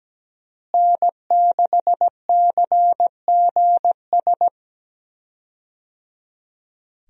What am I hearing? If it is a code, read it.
Morse code, heard as a keyed tone, about 17 wpm, 700 Hz. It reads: N6CGS
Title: USA Call Signs - 5 Characters - Repeated 3x - 17wpm